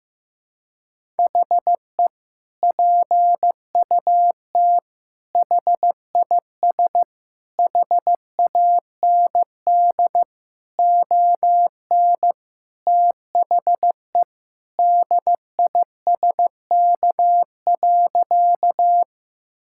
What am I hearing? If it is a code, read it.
Morse code, heard as a keyed tone, about 15 wpm, 700 Hz. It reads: HE PUT HIS HAND ON THE DISK.
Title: Sentences from Top 1000 Words - Review Part 6 - 15wpm